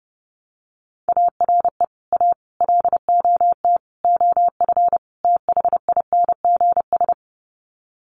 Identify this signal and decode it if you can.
Morse code, heard as a keyed tone, about 30 wpm, 700 Hz. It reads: ARE A LOT OF THINGS
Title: Sets of 5 Words Encore2 30wpm